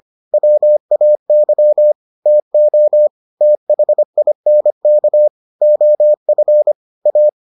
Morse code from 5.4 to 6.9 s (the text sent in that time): OF